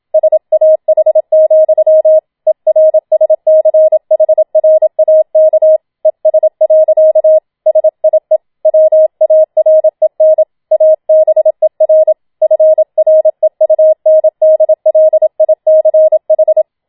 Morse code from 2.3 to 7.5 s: ERSCHRAK ES.